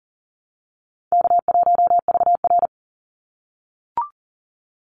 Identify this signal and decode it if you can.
Morse code, heard as a keyed tone, about 40 wpm, 700 Hz. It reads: K1VR
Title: Easy Call Signs - 40wpm